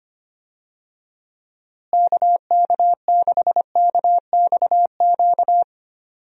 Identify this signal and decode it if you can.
Morse code, heard as a keyed tone, about 25 wpm, 700 Hz. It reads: KK6KXQ
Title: USA Call Signs - 6 Characters - Repeated 2x - 25wpm